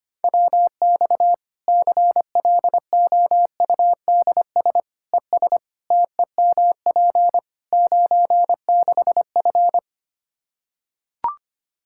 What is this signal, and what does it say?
Morse code, heard as a keyed tone, about 25 wpm, 700 Hz. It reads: WX CLOUDS ES TEMP 96F
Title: Practice QSO - Exchange 2 - Info - 25wpm